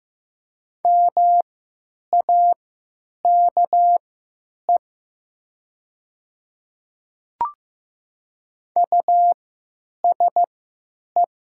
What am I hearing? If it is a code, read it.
Morse code, heard as a keyed tone, about 15 wpm, spaced as for 9 wpm, 700 Hz. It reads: MAKE USE
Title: 3x Character Spacing - Top 200 Words - 15wpm